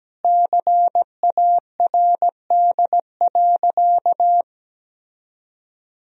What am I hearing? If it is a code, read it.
Morse code, heard as a keyed tone, about 17 wpm, 700 Hz. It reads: CARD.
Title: Sentences from Top 1000 Words - Review Part 4 - 17wpm